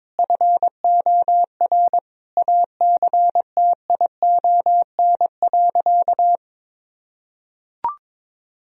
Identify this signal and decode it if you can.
Morse code, heard as a keyed tone, about 22 wpm, 700 Hz. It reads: FOR ACTION.
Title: Sentences from Top 1000 Words - Review Part 4 - 22wpm